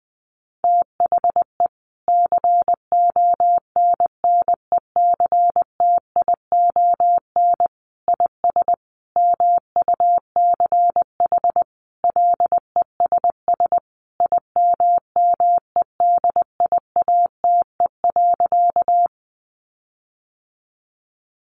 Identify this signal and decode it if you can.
Morse code, heard as a keyed tone, about 20 wpm, 700 Hz. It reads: THE CONNECTION IS MUCH LESS IMMEDIATE.
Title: Sentences from Top 1300 Words 20wpm